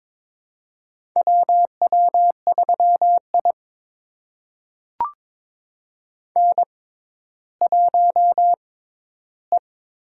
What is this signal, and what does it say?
Morse code, heard as a keyed tone, about 22 wpm, spaced as for 13 wpm, 700 Hz. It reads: WW3I N1E
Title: USA Call Signs - 4 Characters - 6x Character Spacing - 22wpm